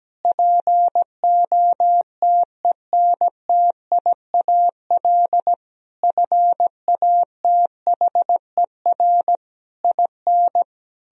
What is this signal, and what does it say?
Morse code, heard as a keyed tone, about 17 wpm, 700 Hz. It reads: POTENTIAL FATHER IN
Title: Sentences from Top 1000 Words - Review Part 4 - 17wpm